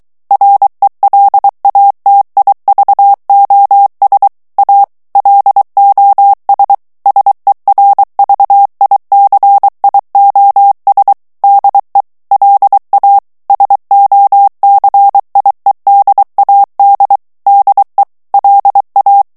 Morse code, heard as a keyed tone, about 23 wpm, 800 Hz. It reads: RELATIVOS A LOS SERVICIOS DE LA SOCIEDAD DE LA
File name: morse mp3 / ESPAGNOL TEXTE EN CLAIR 2 VITESSE 22 MOTS MINUTES